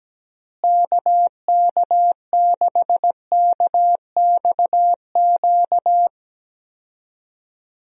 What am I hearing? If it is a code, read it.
Morse code, heard as a keyed tone, about 17 wpm, 700 Hz. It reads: KK6KXQ
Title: USA Call Signs - 6 Characters - Repeated 3x - 17wpm